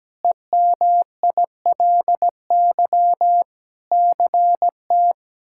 Morse code, heard as a keyed tone, about 17 wpm, 700 Hz. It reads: EMILY CT
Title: K1USN SST QSO Practice - 17wpm